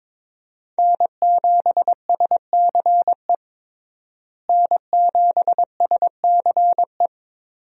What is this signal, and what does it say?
Morse code, heard as a keyed tone, about 22 wpm, 700 Hz. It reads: N7SCE N7SCE
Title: USA Call Signs - 5 Characters - Repeated 3x - 22wpm